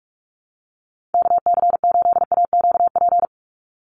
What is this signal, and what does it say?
Morse code, heard as a keyed tone, about 44 wpm, 700 Hz. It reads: KC8AQP
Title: USA Call Signs 45wpm